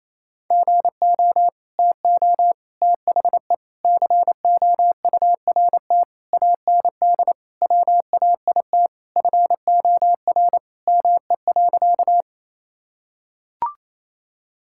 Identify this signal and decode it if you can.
Morse code, heard as a keyed tone, about 28 wpm, 700 Hz. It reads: GO TO THE COURT AND WAIT FOR ME.
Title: Sentences from Top 1600 Words 28wpm